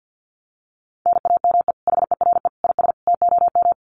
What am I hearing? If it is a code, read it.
Morse code, heard as a keyed tone, about 50 wpm, 700 Hz. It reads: NAME HERE IS TOM
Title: Common QSO Phrases 50wpm